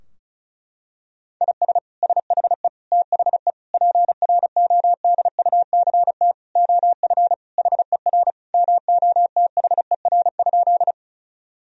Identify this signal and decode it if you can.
Morse code, heard as a keyed tone, about 34 wpm, 700 Hz. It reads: IS SHE THE PRODUCT OF HER MOTHER?
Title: Sentences from Top 1000 Words - Review Part 2 - 35wpm